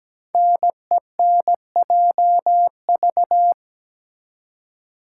Morse code, heard as a keyed tone, about 17 wpm, 700 Hz. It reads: NENJV
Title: Code Groups - 5 Letters - 17wpm